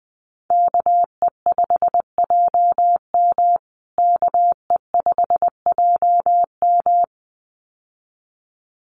Morse code, heard as a keyed tone, about 20 wpm, 700 Hz. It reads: KE5JM KE5JM
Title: USA Call Signs Repeated 2x 20wpm